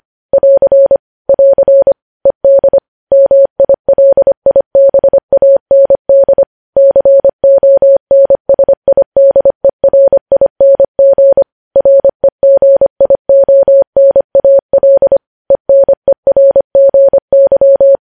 Morse code, heard as a keyed tone, about 25 wpm, 550 Hz. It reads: <AR> <AR> ED MILIBAND CONSIDERING REGIONAL ENERGY
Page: News Headlines in Morse Code at 25 WPM